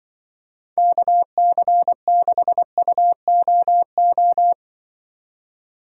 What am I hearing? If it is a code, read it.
Morse code, heard as a keyed tone, about 24 wpm, 700 Hz. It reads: KC6UOO
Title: Speed Racing - USA 2x3 Call Signs 20wpm